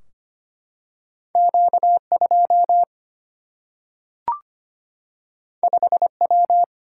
Morse code, heard as a keyed tone, about 25 wpm, 700 Hz. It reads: Q2 5W
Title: Single-Letter Single-Number 25wpm